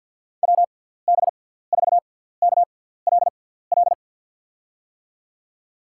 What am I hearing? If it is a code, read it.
Morse code, heard as a keyed tone, about 48 wpm, spaced as for 17 wpm, 700 Hz. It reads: WB4KLL
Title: USA Call Signs - 6x Character Spacing - 50wpm